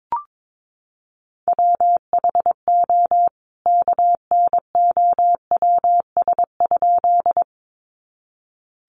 Morse code, heard as a keyed tone, about 22 wpm, 700 Hz. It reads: WHO KNOWS?
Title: A Sound Of Thunder 22wpm